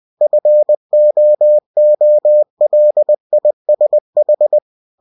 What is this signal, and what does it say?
Morse code, heard as a keyed tone, about 20 wpm, 600 Hz. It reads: FOOLISH
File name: cw-books / winnie-the-pooh / winnie-20wpm-page-0025